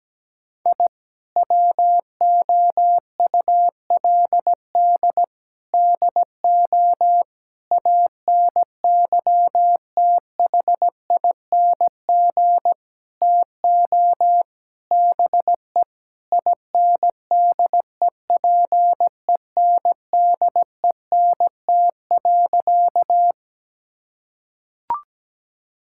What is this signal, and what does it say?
Morse code, heard as a keyed tone, about 17 wpm, 700 Hz. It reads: I WOULD DO ANYTHING TO BE INDEPENDENT.
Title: Sentences from Top 1000 Words - Review Part 6 - 17wpm